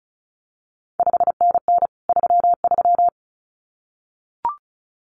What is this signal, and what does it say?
Morse code, heard as a keyed tone, about 36 wpm, 700 Hz. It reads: 5NN 33
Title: CQ WW DX Contest Exchange Encore - 35wpm